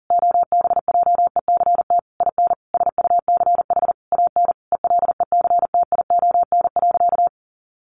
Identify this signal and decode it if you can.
Morse code, heard as a keyed tone, about 40 wpm, 700 Hz. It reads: OBJECT IN SUCH AN ELECTION.
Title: Sentences from Top 1500 Words 40wpm